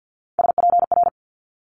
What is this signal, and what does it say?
Morse code, heard as a keyed tone, about 50 wpm, 700 Hz. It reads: SPR
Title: Multi-letter Phonemes Quick Fire 50wpm